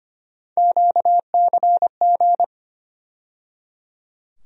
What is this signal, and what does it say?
Morse code, heard as a keyed tone, about 25 wpm, 700 Hz. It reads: QCG
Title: USA Call Sign Postfixes - 3 Letters 25wpm